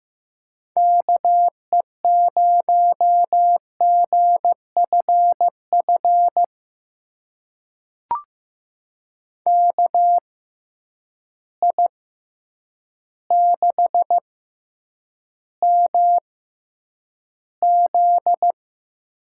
Morse code, heard as a keyed tone, about 15 wpm, spaced as for 8 wpm, 700 Hz. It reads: KE0GFF KI6MZ
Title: USA Call Signs - 6 Characters - 6x Character Spacing - 15wpm